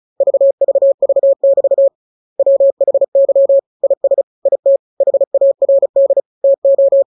Morse code, heard as a keyed tone, about 34 wpm, 550 Hz. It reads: VVV<BT> WHY IS IT HARD TO
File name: jokes / joke 9 35 0000